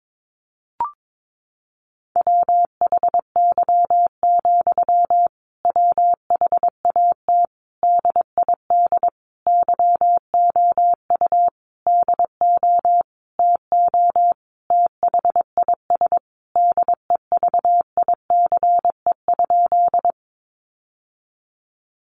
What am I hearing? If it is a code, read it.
Morse code, heard as a keyed tone, about 22 wpm, 700 Hz. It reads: WHY, WHAT DID YOU DO TO THIS DEVICE?
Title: Sentences from Top 1500 Words 22wpm